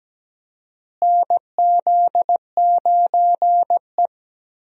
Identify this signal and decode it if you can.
Morse code, heard as a keyed tone, about 17 wpm, 700 Hz. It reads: NZ9E
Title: USA Call Signs - 4 Characters - Repeated 3x - 17wpm